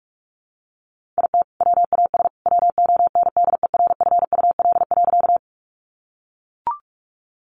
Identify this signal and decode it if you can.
Morse code, heard as a keyed tone, about 44 wpm, 700 Hz. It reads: IT WAS WONDERFUL.
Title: Sentences from Top 1100 Words 45wpm